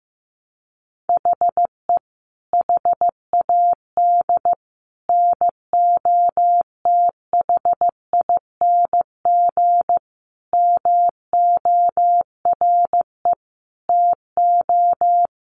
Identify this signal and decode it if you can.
Morse code, heard as a keyed tone, about 15 wpm, 700 Hz. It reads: HE HAD NOTHING MORE TO
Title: Sentences from Top 1100 Words 15wpm